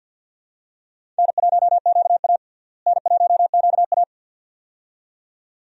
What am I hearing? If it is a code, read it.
Morse code, heard as a keyed tone, about 52 wpm, 700 Hz. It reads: N1QA N1QA
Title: USA Call Signs - 4 Characters - Repeated 2x - 50wpm